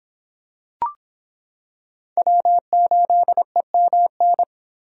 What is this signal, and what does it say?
Morse code, heard as a keyed tone, about 26 wpm, 700 Hz. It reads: W8EMN